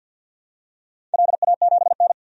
Morse code, heard as a keyed tone, about 46 wpm, 700 Hz. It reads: LA7N
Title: DXCC-100 Call Signs - 4 Characters - 50wpm